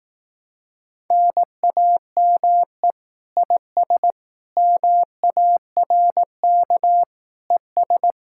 Morse code, heard as a keyed tone, about 18 wpm, 700 Hz. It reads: NAME IS MARK ES
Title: Speed-Racing Common QSO Phrases Encore - 15wpm